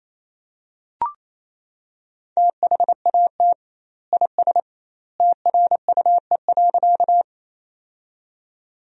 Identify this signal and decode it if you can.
Morse code, heard as a keyed tone, about 28 wpm, 700 Hz. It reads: THAT IS TRUE.
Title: 2x Word-Spacing - Sentences from Top 400 Words - 28wpm